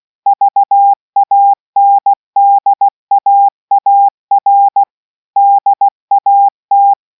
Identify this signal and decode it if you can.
Morse code, heard as a keyed tone, about 16 wpm, 800 Hz. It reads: VANDAAR DAT